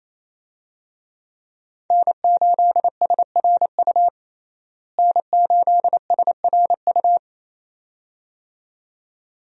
Morse code, heard as a keyed tone, about 28 wpm, 700 Hz. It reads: N8SRU N8SRU
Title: USA Call Signs - 5 Characters - Repeated 2x - 28wpm